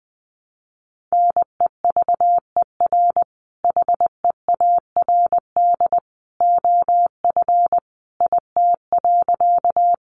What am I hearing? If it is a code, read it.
Morse code, heard as a keyed tone, about 19 wpm, 700 Hz. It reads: NEVER HEARD OF IT.
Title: Sentences from Top 400 Words 20wpm